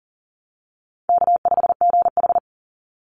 Morse code, heard as a keyed tone, about 37 wpm, 700 Hz. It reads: K5GH
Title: Call Signs Repeated 3x - 40wpm